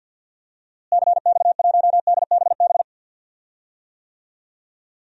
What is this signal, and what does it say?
Morse code, heard as a keyed tone, about 47 wpm, 700 Hz. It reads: CX1DBB
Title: DXCC-100 Call Signs - 6 Characters - Repeated 2x - 50wpm